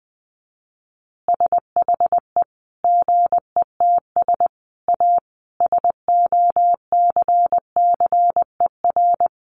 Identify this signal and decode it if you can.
Morse code, heard as a keyed tone, about 20 wpm, 700 Hz. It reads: SHE GETS A SOCCER
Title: Easy Phrases 20wpm